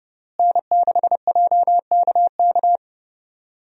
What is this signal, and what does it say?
Morse code, heard as a keyed tone, about 32 wpm, 700 Hz. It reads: N6JKK
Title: USA Call Signs - 5 Characters - Repeated 3x - 30wpm